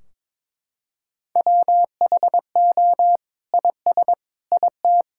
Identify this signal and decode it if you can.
Morse code, heard as a keyed tone, about 22 wpm, 700 Hz. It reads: WHO IS IT
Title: Sentences from Top 1600 Words 22wpm